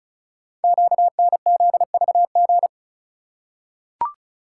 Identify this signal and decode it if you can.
Morse code, heard as a keyed tone, about 36 wpm, 700 Hz. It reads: QNZVG
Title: Code Groups - 5 Letters - 35wpm